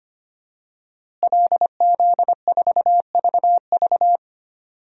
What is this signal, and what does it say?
Morse code, heard as a keyed tone, about 25 wpm, 700 Hz. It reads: LZ4VV
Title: DXCC-100 Call Signs - 5 Characters - Repeated 2x - 25wpm